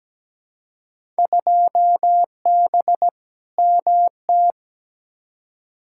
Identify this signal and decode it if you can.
Morse code, heard as a keyed tone, about 17 wpm, 700 Hz. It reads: 2B MT
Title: Field Day Exchanges Encore - 17wpm